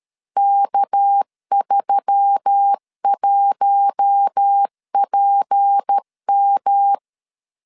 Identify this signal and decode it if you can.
Morse code, heard as a keyed tone, about 13 wpm, 800 Hz. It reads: K31PM
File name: LEZIONI / 6 PKX0 / 6 PKX0 12wpm 2231